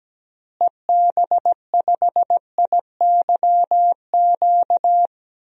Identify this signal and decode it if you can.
Morse code, heard as a keyed tone, about 17 wpm, 700 Hz. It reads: EB5IYQ
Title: DXCC-100 Call Signs - 6 Characters - Repeated 2x - 17wpm